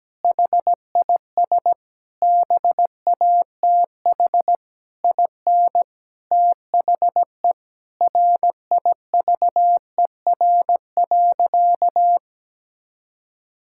Morse code, heard as a keyed tone, about 17 wpm, 700 Hz. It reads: HIS BATH IN THE RIVER.
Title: Sentences from Top 1600 Words 17wpm